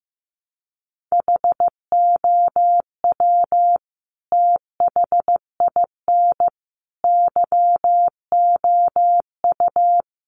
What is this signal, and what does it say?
Morse code, heard as a keyed tone, about 15 wpm, 700 Hz. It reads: HOW THIN YOU
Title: Sentences from Top 1200 Words 15wpm